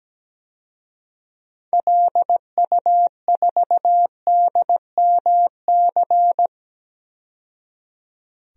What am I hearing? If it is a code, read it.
Morse code, heard as a keyed tone, about 17 wpm, 700 Hz. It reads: LU4DMC
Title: DXCC-100 Call Signs - 6 Characters - 17wpm